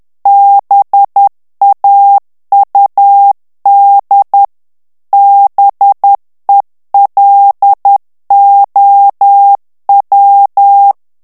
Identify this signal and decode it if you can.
Morse code, heard as a keyed tone, about 10 wpm, 800 Hz. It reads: BAUD BELOW
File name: ANGLAIS texte clair 3 vitesse 10 mots minute